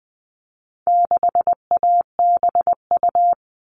Morse code, heard as a keyed tone, about 20 wpm, 700 Hz. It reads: 6ABU
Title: All Cut Numbers - 4-Digit 20wpm